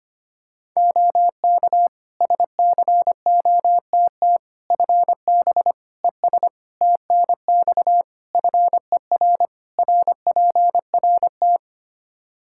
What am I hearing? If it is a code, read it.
Morse code, heard as a keyed tone, about 25 wpm, 700 Hz. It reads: OK SCOTT FB ES TNX FER RPRT
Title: Practice QSO - Exchange 2 - Info - 25wpm